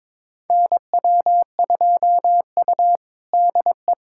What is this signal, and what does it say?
Morse code, heard as a keyed tone, about 22 wpm, 700 Hz. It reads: NW2U DE
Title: USA Callsign DE Callsign 22wpm